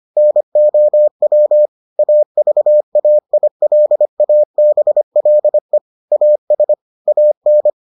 Morse code, heard as a keyed tone, about 25 wpm, 600 Hz. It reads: NOW AVAILABLE AS AN